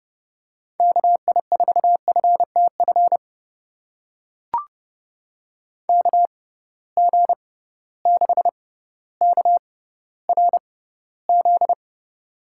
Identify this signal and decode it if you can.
Morse code, heard as a keyed tone, about 30 wpm, spaced as for 15 wpm, 700 Hz. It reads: KI4FTF KG6KRZ
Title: USA Call Signs - 6 Characters - 6x Character Spacing - 30wpm